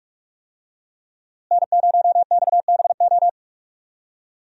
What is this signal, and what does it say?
Morse code, heard as a keyed tone, about 46 wpm, 700 Hz. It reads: N0XBO
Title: USA Call Signs - 5 Characters - Repeated 2x - 45wpm